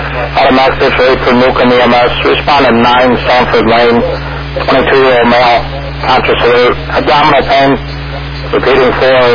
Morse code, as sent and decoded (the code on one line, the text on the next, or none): none